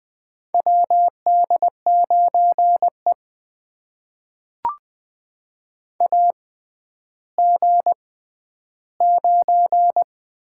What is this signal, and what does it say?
Morse code, heard as a keyed tone, about 20 wpm, spaced as for 11 wpm, 700 Hz. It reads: WD9E AG9
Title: USA Call Signs - 4 Characters - 6x Character Spacing - 20wpm